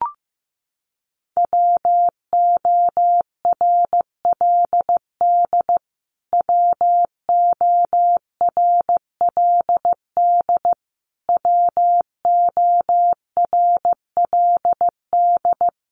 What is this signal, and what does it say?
Morse code, heard as a keyed tone, about 15 wpm, 700 Hz. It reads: WORLD WORLD WORLD
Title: Top 200 Words Repeated 15wpm